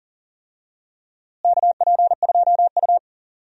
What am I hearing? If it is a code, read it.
Morse code, heard as a keyed tone, about 37 wpm, 700 Hz. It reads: KP2U
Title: DXCC-100 Call Signs - 4 Characters - Repeated 2x - 40wpm